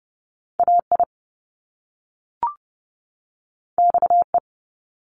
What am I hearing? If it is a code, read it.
Morse code, heard as a keyed tone, about 30 wpm, 700 Hz. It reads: AI XE
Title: Top Two Letter Combinations in Words 30wpm